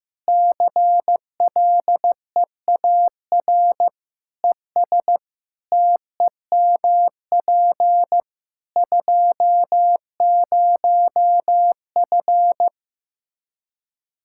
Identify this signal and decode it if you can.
Morse code, heard as a keyed tone, about 15 wpm, 700 Hz. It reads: CLEAR ES TEMP 20F
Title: Practice QSO - Exchange 2 - Info - 15wpm